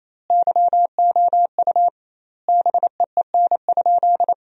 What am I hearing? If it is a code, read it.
Morse code, heard as a keyed tone, about 28 wpm, 700 Hz. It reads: YOU BEEN?
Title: 2x Word-Spacing - Sentences from Top 200 Words - 28wpm